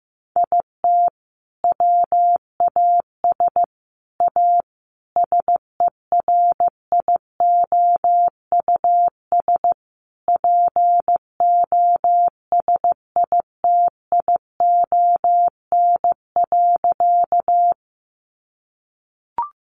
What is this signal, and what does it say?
Morse code, heard as a keyed tone, about 15 wpm, 700 Hz. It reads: IT WAS A SERIOUS POSITION.
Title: Sentences from Top 1100 Words 15wpm